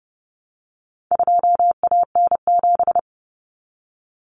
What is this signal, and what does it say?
Morse code, heard as a keyed tone, about 30 wpm, 700 Hz. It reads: 2AN7
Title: Common Cut Numbers - 4-Digit 30wpm